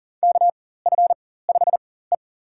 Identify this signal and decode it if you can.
Morse code, heard as a keyed tone, about 40 wpm, spaced as for 19 wpm, 700 Hz. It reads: KF5E
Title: USA Call Signs - 4 Characters - 4x Character Spacing - 40wpm